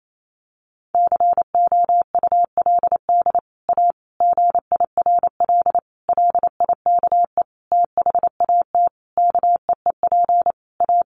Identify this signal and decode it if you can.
Morse code, heard as a keyed tone, about 28 wpm, 700 Hz. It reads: COULD A GIRL LIKE THAT KEEP A
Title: Sentences from Top 1200 Words 28wpm